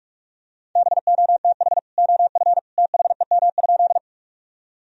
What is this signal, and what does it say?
Morse code, heard as a keyed tone, about 41 wpm, 700 Hz. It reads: BOTH OF THEM?
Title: Sentences from Top 1000 Words - Review Part 1 - 45wpm